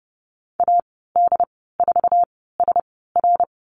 Morse code, heard as a keyed tone, about 30 wpm, spaced as for 17 wpm, 700 Hz. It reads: AD4SR
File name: Lesson 15g - Number 5 - Call Signs 30wpm